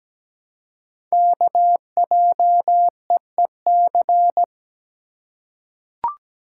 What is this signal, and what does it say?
Morse code, heard as a keyed tone, about 17 wpm, 700 Hz. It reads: KJEEC
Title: Code Groups - 5 Letters - 17wpm